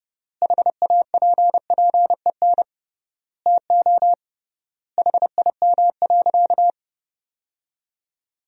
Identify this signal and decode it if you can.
Morse code, heard as a keyed tone, about 30 wpm, 700 Hz. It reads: HAPPEN TO HIM.
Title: 3x Word-Spacing - Sentences from Top 400 Words - 30wpm